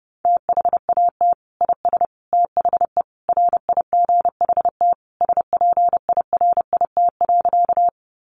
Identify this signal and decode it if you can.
Morse code, heard as a keyed tone, about 30 wpm, 700 Hz. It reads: THAT IS THE RIGHT SPIRIT.
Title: Sentences from Top 1400 Words 30wpm